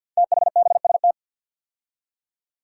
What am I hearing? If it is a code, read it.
Morse code, heard as a keyed tone, about 47 wpm, 700 Hz. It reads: THBST
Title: Code Groups - 5 Letters - 50wpm